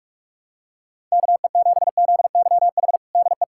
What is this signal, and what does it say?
Morse code, heard as a keyed tone, about 41 wpm, 700 Hz. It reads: KE7ZYH DE
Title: USA Callsign DE Callsign - 6 Characters - 45wpm